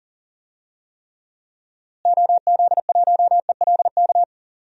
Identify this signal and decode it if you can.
Morse code, heard as a keyed tone, about 39 wpm, 700 Hz. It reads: OZ1ELK
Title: DXCC-100 Call Signs - 6 Characters - Repeated 2x - 40wpm